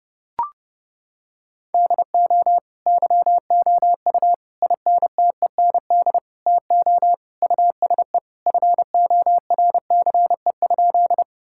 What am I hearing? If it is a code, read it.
Morse code, heard as a keyed tone, about 30 wpm, 700 Hz. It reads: DO YOU INTEND TO USE FORCE?
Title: Sentences from Top 1600 Words 30wpm